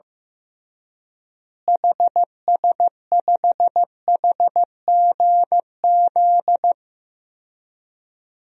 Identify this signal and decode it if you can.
Morse code, heard as a keyed tone, about 15 wpm, 700 Hz. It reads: HS5HGZ
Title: DXCC-100 Call Signs - 6 Characters - Repeated 3x - 15wpm